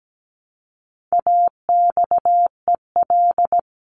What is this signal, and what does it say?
Morse code, heard as a keyed tone, about 17 wpm, 700 Hz. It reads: AXEL
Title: CWT Exchange 17wpm